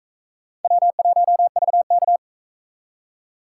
Morse code, heard as a keyed tone, about 42 wpm, 700 Hz. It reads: W1VK